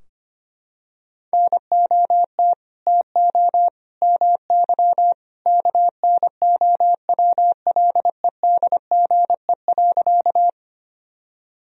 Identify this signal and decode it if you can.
Morse code, heard as a keyed tone, about 25 wpm, 700 Hz. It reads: NOT TO MY KNOWLEDGE.
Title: Sentences from Top 1000 Words - Review Part 3 - 25wpm